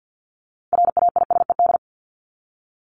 Morse code, heard as a keyed tone, about 48 wpm, 700 Hz. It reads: RAISED
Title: ED Suffix - Top 100 Words with Base Words 50wpm